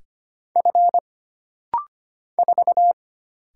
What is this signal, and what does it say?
Morse code, heard as a keyed tone, about 25 wpm, 700 Hz. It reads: F 4
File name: Single Character (Letter:Number:Punctuation:Prosign) - Rapid-Fire - 25wpm